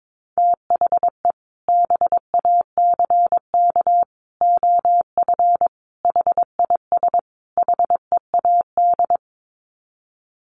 Morse code, heard as a keyed tone, about 22 wpm, 700 Hz. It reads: THE BACK OF HIS HEAD
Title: Sets of 5 Words Encore 22wpm